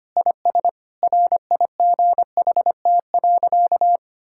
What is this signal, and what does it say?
Morse code, heard as a keyed tone, about 25 wpm, 700 Hz. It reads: IS RIGHT.
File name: practice / 25wpm / Sentences with 4-Word Phrases - 25wpm